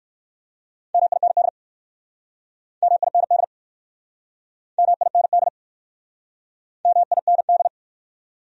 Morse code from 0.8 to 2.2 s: MIND

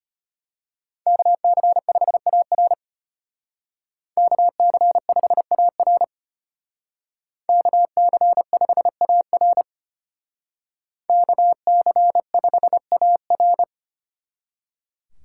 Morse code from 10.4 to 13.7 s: KC5AR

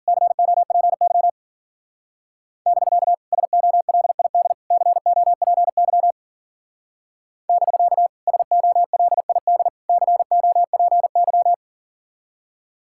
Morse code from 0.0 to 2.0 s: COPY

